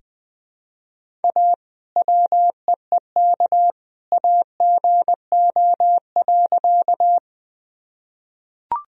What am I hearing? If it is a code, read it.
Morse code, heard as a keyed tone, about 20 wpm, 700 Hz. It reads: A WEEK AGO.
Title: Sentences from Top 1000 Words - Review Part 3 - 20wpm